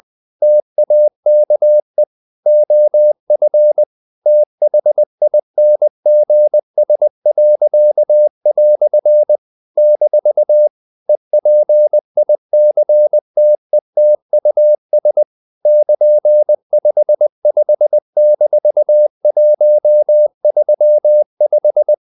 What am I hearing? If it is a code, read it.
Morse code, heard as a keyed tone, about 20 wpm, 600 Hz. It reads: TAKE OF THINGS." - EPICTETUS <KN>55-135
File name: CW / Practice / EnglishQuotations0320